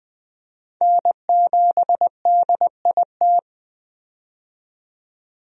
Morse code, heard as a keyed tone, about 20 wpm, 700 Hz. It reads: N7DIT
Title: USA Call Signs - 5 Characters - 20wpm